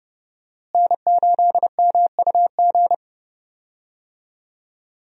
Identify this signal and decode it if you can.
Morse code, heard as a keyed tone, about 30 wpm, 700 Hz. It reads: N8MUG
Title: USA Call Signs - 5 Characters - Repeated 3x - 30wpm